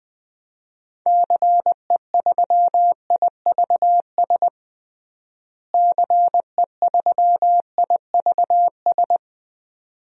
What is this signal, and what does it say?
Morse code, heard as a keyed tone, about 20 wpm, 700 Hz. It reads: CE3IVS CE3IVS
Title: DXCC-100 Call Signs - 6 Characters - Repeated 2x - 20wpm